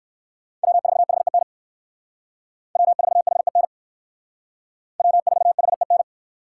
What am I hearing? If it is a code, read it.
Morse code, heard as a keyed tone, about 57 wpm, 700 Hz. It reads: W4HEN W4HEN W4HEN